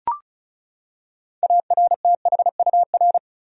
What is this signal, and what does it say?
Morse code, heard as a keyed tone, about 36 wpm, 700 Hz. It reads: ARTHUR
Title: USA Common Names - 35wpm